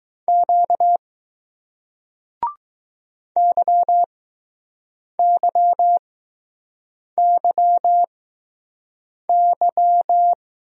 Morse code, heard as a keyed tone, about 21 wpm, spaced as for 6 wpm, 700 Hz. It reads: Q YYYY